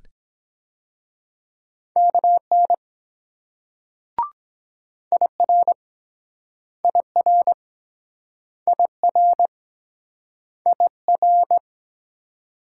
Morse code, heard as a keyed tone, about 23 wpm, 700 Hz. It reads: KN IR IR IR IR